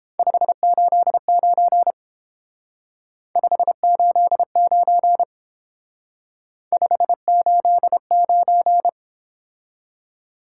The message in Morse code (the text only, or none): 589 589 589